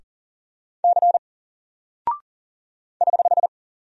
Morse code, heard as a keyed tone, about 40 wpm, 700 Hz. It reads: C <HH>
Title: Single Character (Letter:Number:Punctuation:Prosign) 40wpm